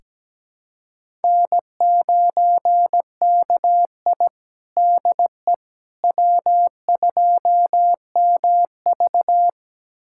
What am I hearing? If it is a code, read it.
Morse code, heard as a keyed tone, about 17 wpm, 700 Hz. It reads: N9KI DE W2MV
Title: USA Callsign DE Callsign - 4 Characters - 17wpm